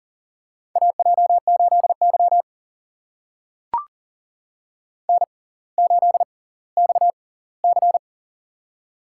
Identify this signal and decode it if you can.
Morse code, heard as a keyed tone, about 40 wpm, spaced as for 20 wpm, 700 Hz. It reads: AJ8Y N8XC
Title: USA Call Signs - 6x Character Spacing - 40wpm